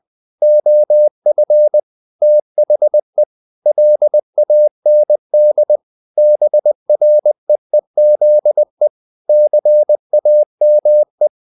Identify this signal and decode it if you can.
Morse code, heard as a keyed tone, about 20 wpm, 600 Hz. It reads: OF THE LAND BREEZE CAME
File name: the old man and the sea / the old man and the sea part 1 20wpm